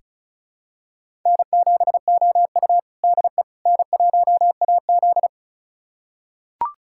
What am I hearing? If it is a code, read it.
Morse code, heard as a keyed tone, about 36 wpm, 700 Hz. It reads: N7OU DE N1AZ
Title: USA Callsign DE Callsign - 4 Characters - 35wpm